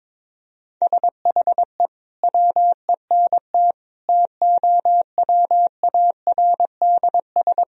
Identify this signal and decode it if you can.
Morse code, heard as a keyed tone, about 22 wpm, 700 Hz. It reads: SHE WENT TOWARDS
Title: Sentences from Top 1000 Words - Review Part 2 - 22wpm